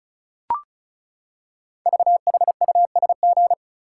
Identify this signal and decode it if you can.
Morse code, heard as a keyed tone, about 36 wpm, 690 Hz. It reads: VHUSG
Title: Code Groups - 5 Letters - 35wpm